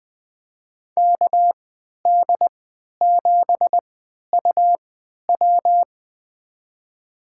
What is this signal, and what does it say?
Morse code, heard as a keyed tone, about 20 wpm, spaced as for 11 wpm, 700 Hz. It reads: KD7UW
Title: USA Call Signs - 5 Characters - 3x Character Spacing - 20wpm